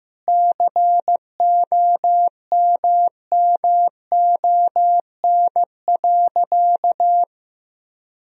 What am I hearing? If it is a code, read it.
Morse code, heard as a keyed tone, about 15 wpm, 700 Hz. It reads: COMMON.
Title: Sentences from Top 1000 Words - Review Part 3 - 15wpm